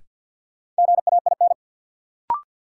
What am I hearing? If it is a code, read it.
Morse code, heard as a keyed tone, about 50 wpm, 700 Hz. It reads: GAIN
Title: Sentences from Top 1000 Words - New Word Drill - 50wpm